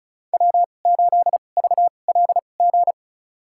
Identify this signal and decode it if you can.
Morse code, heard as a keyed tone, about 36 wpm, spaced as for 25 wpm, 700 Hz. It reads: W8VLG
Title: USA Call Signs - 5 Characters - 2x Character Spacing - 35wpm